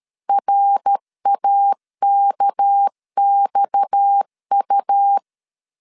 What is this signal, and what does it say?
Morse code, heard as a keyed tone, about 13 wpm, 800 Hz. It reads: RAKXU
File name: LEZIONI / 6 PKX0 / 6 PKX0 12wpm 2231